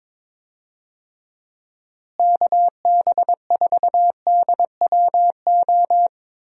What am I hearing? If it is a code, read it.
Morse code, heard as a keyed tone, about 22 wpm, 700 Hz. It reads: KB4DWO
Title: USA Call Signs - 6 Characters - Repeated 2x - 22wpm